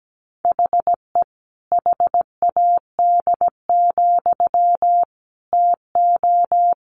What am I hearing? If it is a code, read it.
Morse code, heard as a keyed tone, about 17 wpm, 700 Hz. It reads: HE HAD, TO
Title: Sentences from Top 500 Words 17wpm